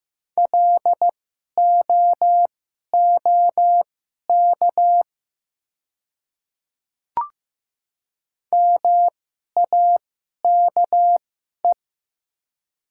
Morse code, heard as a keyed tone, about 15 wpm, spaced as for 11 wpm, 700 Hz. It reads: LOOK MAKE
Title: 2x Character Spacing - Top 100 Words - 15wpm